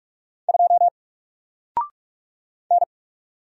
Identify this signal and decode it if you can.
Morse code, heard as a keyed tone, about 44 wpm, 700 Hz. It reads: 2 N
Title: Single Letter-Number 45wpm